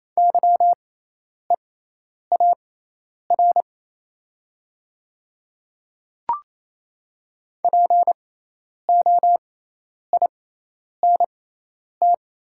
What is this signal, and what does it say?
Morse code, heard as a keyed tone, about 28 wpm, spaced as for 10 wpm, 700 Hz. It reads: YEAR POINT
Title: 6x Character Spacing - Top 500 Words - 28wpm